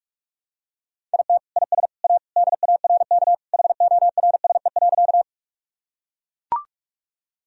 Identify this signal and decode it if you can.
Morse code, heard as a keyed tone, about 43 wpm, 700 Hz. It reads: IT IS A DARK HORSE.
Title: Sentences from Top 1000 Words - Review Part 1 - 45wpm